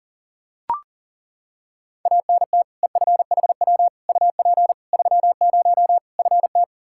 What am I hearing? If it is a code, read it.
Morse code, heard as a keyed tone, about 40 wpm, 700 Hz. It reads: ANT EFHW UP 30 FT
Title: Practice QSO - Exchange 2 - Info - 40wpm